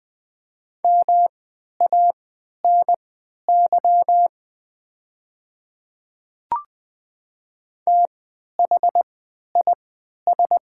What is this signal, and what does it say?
Morse code, heard as a keyed tone, about 20 wpm, spaced as for 11 wpm, 700 Hz. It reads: MANY THIS
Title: 3x Character Spacing - Top 100 Words - 20wpm